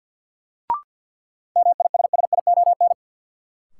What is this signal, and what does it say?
Morse code, heard as a keyed tone, about 52 wpm, 700 Hz. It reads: MISSION
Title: Sentences from Top 1600 Words - New Word Drill Alt - 50wpm